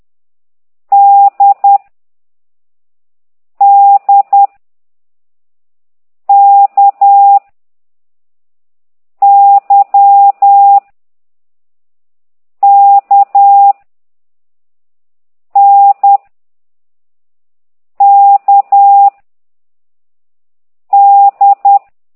Morse code, heard as a keyed tone, about 10 wpm, 800 Hz. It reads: D D K Y K N K D